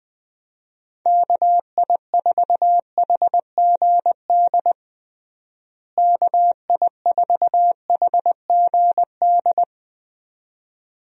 Morse code, heard as a keyed tone, about 20 wpm, 700 Hz. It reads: KI4HGD KI4HGD
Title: USA Call Signs - 6 Characters - Repeated 2x - 20wpm